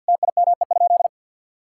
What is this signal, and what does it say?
Morse code, heard as a keyed tone, about 49 wpm, 690 Hz. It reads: TIME?
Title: Sentences from Top 1000 Words - Review Part 5 - 50wpm